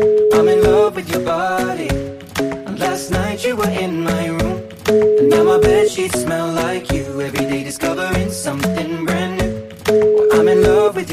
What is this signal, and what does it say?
Ringback tone (Europe, 425 Hz), with noise.